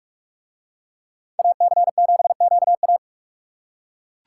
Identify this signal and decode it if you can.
Morse code, heard as a keyed tone, about 41 wpm, 700 Hz. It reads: AC7QA